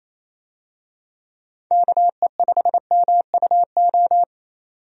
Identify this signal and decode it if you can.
Morse code, heard as a keyed tone, about 28 wpm, 700 Hz. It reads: KE5MUO
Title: USA Call Signs - 6 Characters - 28wpm